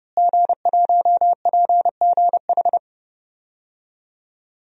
Morse code, heard as a keyed tone, about 30 wpm, 700 Hz. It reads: G1PGH
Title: DXCC-100 Call Signs - 5 Characters - Repeated 2x - 30wpm